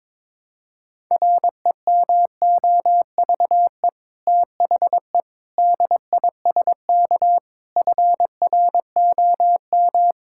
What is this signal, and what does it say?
Morse code, heard as a keyed tone, about 22 wpm, 700 Hz. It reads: REMOVE THE DISK FROM
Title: Sentences from Top 1000 Words - Review Part 6 - 22wpm